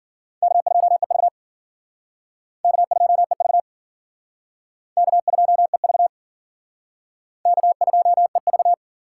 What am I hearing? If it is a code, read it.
Morse code, heard as a keyed tone, about 52 wpm, 700 Hz. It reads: K2EV K2EV K2EV K2EV